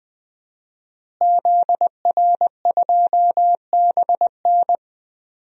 Morse code, heard as a keyed tone, about 20 wpm, 700 Hz. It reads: ZR2BN
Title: DXCC-100 Call Signs - 5 Characters - Repeated 3x - 20wpm